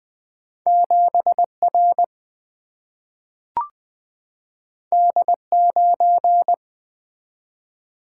Morse code, heard as a keyed tone, about 20 wpm, 700 Hz. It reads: 7 R D 9
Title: Single-Letter Single-Number 20wpm